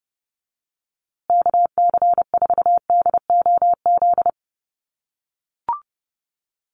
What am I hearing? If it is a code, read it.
Morse code, heard as a keyed tone, about 29 wpm, 700 Hz. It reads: KC4DOZ
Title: USA Call Signs Repeated 3x 30wpm